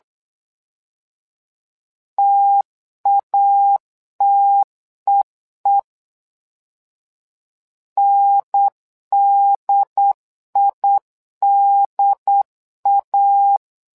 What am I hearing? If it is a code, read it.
Morse code, heard as a keyed tone, about 8 wpm, 800 Hz. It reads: TATEE NDIDA
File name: LEZIONI / 2 UHDN / 2 UHDN 8wpm2232